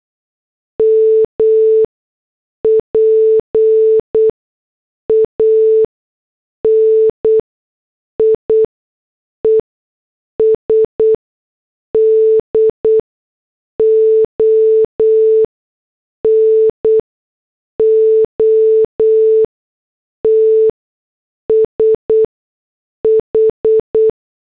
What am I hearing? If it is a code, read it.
Morse code, heard as a keyed tone, about 8 wpm, spaced as for 6 wpm, 440 Hz. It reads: MPANIES DO NOT SH
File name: morse / headlines